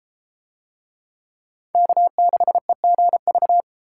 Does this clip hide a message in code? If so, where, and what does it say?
Yes: Morse code from 1.7 to 3.6 s, heard as a keyed tone, about 32 wpm, 700 Hz. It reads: K6EGV